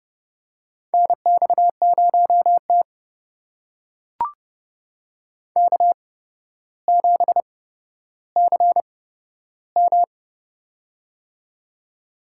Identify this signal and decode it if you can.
Morse code, heard as a keyed tone, about 30 wpm, spaced as for 13 wpm, 700 Hz. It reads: NX0T K7CM
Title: USA Call Signs - 4 Characters - 8x Character Spacing - 30wpm